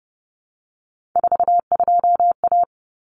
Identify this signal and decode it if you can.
Morse code, heard as a keyed tone, about 30 wpm, 700 Hz. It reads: 42A
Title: Common Cut Numbers - 3-Digit 30wpm